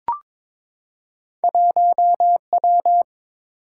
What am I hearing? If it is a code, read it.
Morse code, heard as a keyed tone, about 22 wpm, 700 Hz. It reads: 1W
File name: practice / 22wpm / Two Letter-Number Combinations - 22wpm